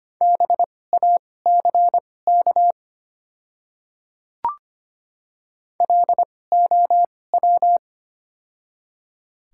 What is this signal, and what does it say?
Morse code, heard as a keyed tone, about 25 wpm, spaced as for 18 wpm, 700 Hz. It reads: BACK LOW
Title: 2x Character Spacing - Top 300 Words - 25wpm